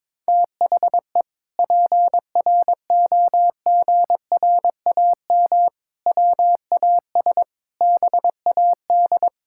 Morse code, heard as a keyed tone, about 21 wpm, 700 Hz. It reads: THE PROGRAM WAS BAD
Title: Sentences from Top 1000 Words - Review Part 5 - 22wpm